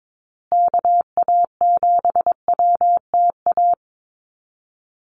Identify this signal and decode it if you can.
Morse code, heard as a keyed tone, about 22 wpm, 700 Hz. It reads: KA7WTA
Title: USA Call Signs Repeated 3x 22wpm